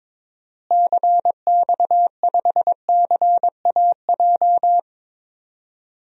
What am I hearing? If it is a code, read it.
Morse code, heard as a keyed tone, about 22 wpm, 700 Hz. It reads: CX5CAJ
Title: DXCC-100 Call Signs - 6 Characters - Repeated 2x - 22wpm